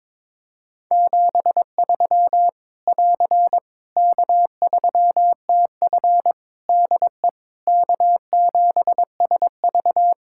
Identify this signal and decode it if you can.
Morse code, heard as a keyed tone, about 22 wpm, 700 Hz. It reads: 73 <AR> K3TF DE K7SV
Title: Practice QSO - Exchange 4 - Ending - 22wpm